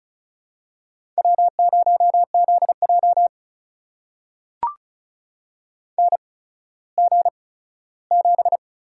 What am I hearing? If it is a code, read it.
Morse code, heard as a keyed tone, about 36 wpm, spaced as for 17 wpm, 700 Hz. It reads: W0ZJ NG7
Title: USA Call Signs - 4 Characters - 8x Character Spacing - 35wpm